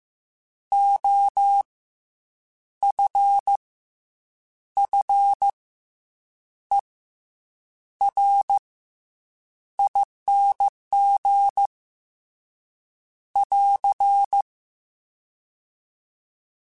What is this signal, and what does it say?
Morse code, heard as a keyed tone, about 15 wpm, 800 Hz. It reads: O F F E R ING <AR>